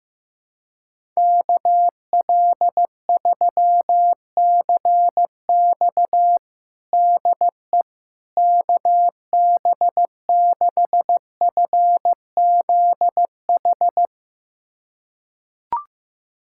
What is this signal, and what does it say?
Morse code, heard as a keyed tone, about 15 wpm, 700 Hz. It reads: KL3CX DE KB6FZH
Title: USA Callsign DE Callsign 15wpm